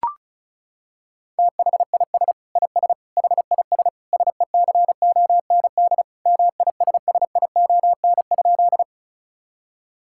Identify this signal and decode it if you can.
Morse code, heard as a keyed tone, about 36 wpm, 700 Hz. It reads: THIS IS HIS SECOND MISSION?
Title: Sentences from Top 1600 Words 35wpm